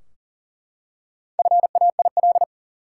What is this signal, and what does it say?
Morse code, heard as a keyed tone, about 40 wpm, 700 Hz. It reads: FAIL
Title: Four Letter Words - 40wpm